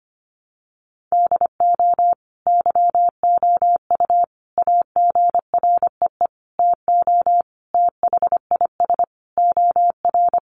Morse code, heard as a keyed tone, about 25 wpm, 700 Hz. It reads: DO YOU AGREE TO THIS OR
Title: Sentences from Top 1100 Words 25wpm